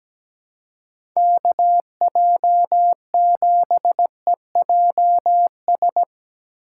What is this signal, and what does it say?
Morse code, heard as a keyed tone, about 17 wpm, 700 Hz. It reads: KJ7EJS
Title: USA Call Signs - 6 Characters - 2x Character Spacing - 17wpm